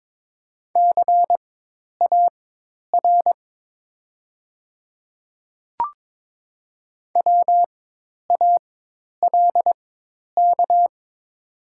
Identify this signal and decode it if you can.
Morse code, heard as a keyed tone, about 22 wpm, spaced as for 10 wpm, 690 Hz. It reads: CAR WALK
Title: 4x Character Spacing - Top 300 Words - 22wpm